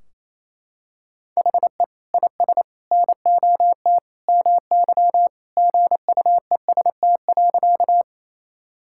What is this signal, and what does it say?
Morse code, heard as a keyed tone, about 28 wpm, 700 Hz. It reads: HE IS NOT MY GUEST.
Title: Sentences from Top 1600 Words 28wpm